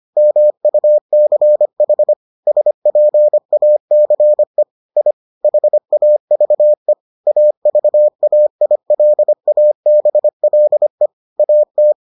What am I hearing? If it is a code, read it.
Morse code, heard as a keyed tone, about 25 wpm, 600 Hz. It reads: MUCH SPACE I HAVE AVAILABLE AT